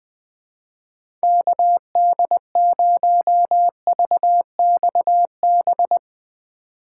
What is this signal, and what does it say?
Morse code, heard as a keyed tone, about 20 wpm, 700 Hz. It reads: KD0VXB
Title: USA Call Signs - 6 Characters - 2x Character Spacing - 20wpm